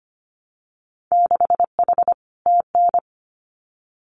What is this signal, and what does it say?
Morse code, heard as a keyed tone, about 25 wpm, 700 Hz. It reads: 6H TN
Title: Winter Field Day Exchanges Encore 25wpm